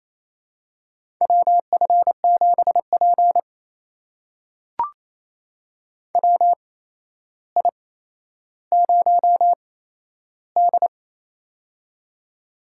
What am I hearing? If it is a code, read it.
Morse code, heard as a keyed tone, about 27 wpm, spaced as for 12 wpm, 700 Hz. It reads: WF7P WI0D
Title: USA Call Signs - 4 Characters - 8x Character Spacing - 28wpm